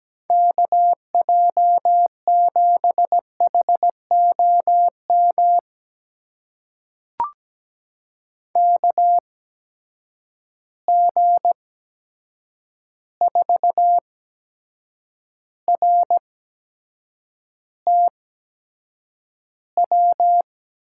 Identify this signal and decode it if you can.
Morse code, heard as a keyed tone, about 17 wpm, spaced as for 7 wpm, 700 Hz. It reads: KJ7HOM KG4RTW
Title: USA Call Signs - 6 Characters - 8x Character Spacing - 17wpm